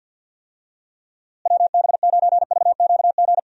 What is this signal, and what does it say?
Morse code, heard as a keyed tone, about 50 wpm, 700 Hz. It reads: WB9VQG
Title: USA Call Signs - 6 Characters - Repeated 2x - 50wpm